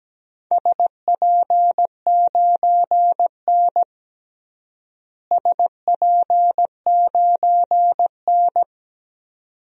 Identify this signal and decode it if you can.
Morse code, heard as a keyed tone, about 17 wpm, 700 Hz. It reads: SP9N SP9N
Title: DXCC-100 Call Signs - 4 Characters - Repeated 2x - 17wpm